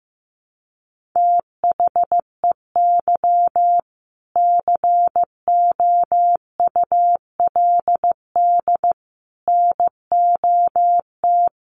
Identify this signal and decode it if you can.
Morse code, heard as a keyed tone, about 15 wpm, 700 Hz. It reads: THEY COULD NOT
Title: Sentences from Top 1100 Words 15wpm